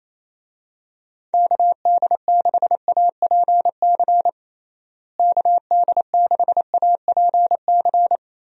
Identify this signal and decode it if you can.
Morse code, heard as a keyed tone, about 28 wpm, 700 Hz. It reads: KD6APC KD6APC
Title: USA Call Signs - 6 Characters - Repeated 2x - 28wpm